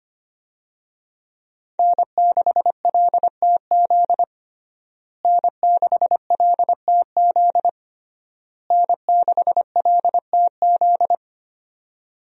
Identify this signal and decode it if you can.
Morse code, heard as a keyed tone, about 25 wpm, 700 Hz. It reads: N6LTZ N6LTZ N6LTZ
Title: USA Call Signs - 5 Characters - Repeated 3x - 25wpm